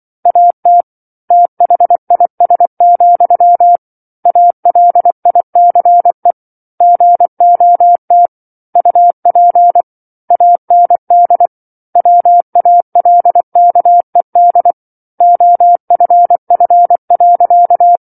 Morse code, heard as a keyed tone, about 24 wpm, 700 Hz. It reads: AT THIS, ALICE GOT UP AND WALKED OFF.
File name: 07-Chapter-07 24 WPM